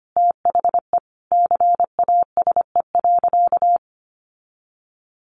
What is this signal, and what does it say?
Morse code, heard as a keyed tone, about 25 wpm, 700 Hz. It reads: THE CASE.
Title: Sentences from Top 1400 Words 25wpm